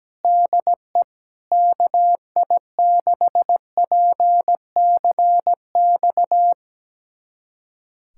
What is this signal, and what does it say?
Morse code, heard as a keyed tone, about 17 wpm, 700 Hz. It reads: DE KI6PCX
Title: USA Callsign DE Callsign - 6 Characters - 17wpm